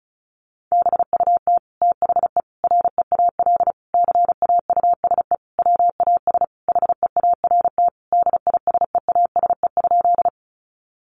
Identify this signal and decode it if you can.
Morse code, heard as a keyed tone, about 34 wpm, 700 Hz. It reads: BUT THE REAL CAUSE WAS HEART DISEASE?
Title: Sentences from Top 1200 Words 35wpm